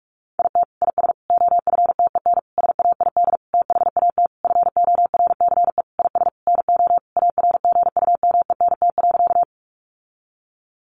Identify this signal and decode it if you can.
Morse code, heard as a keyed tone, about 45 wpm, 700 Hz. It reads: IT IS OFTEN SAID THAT FORCE IS NO ARGUMENT.
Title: Sentences from Top 1400 Words 45wpm